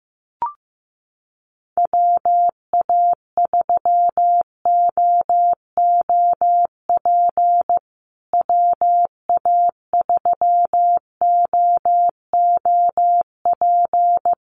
Morse code, heard as a keyed tone, about 15 wpm, 700 Hz. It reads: WA3OOP WA3OOP
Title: USA Call Signs Repeated 3x 15wpm